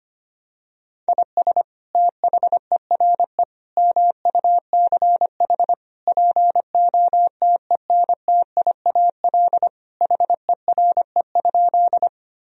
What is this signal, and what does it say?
Morse code, heard as a keyed tone, about 25 wpm, 700 Hz. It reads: IS THERE MUCH POTENTIAL HERE?
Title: Sentences from Top 1000 Words - Review Part 4 - 25wpm